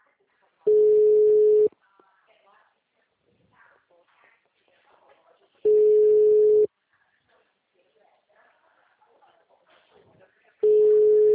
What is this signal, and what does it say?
Ringback tone (Europe, 425 Hz), with speech.